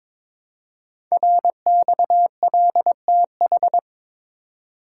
Morse code, heard as a keyed tone, about 23 wpm, 700 Hz. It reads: RXLTH
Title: Code Groups - 5 Letters - 22wpm